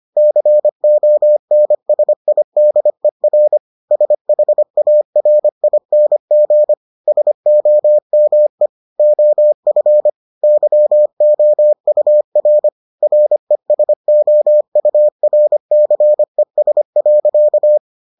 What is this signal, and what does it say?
Morse code, heard as a keyed tone, about 25 wpm, 600 Hz. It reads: CONSIDER SHARING SOME OF YOUR RESOURCES.